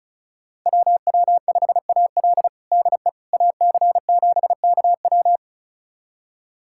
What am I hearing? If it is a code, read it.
Morse code, heard as a keyed tone, about 34 wpm, 700 Hz. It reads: WW5AL DE AC7KW
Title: USA Callsign DE Callsign - 5 Characters - 35wpm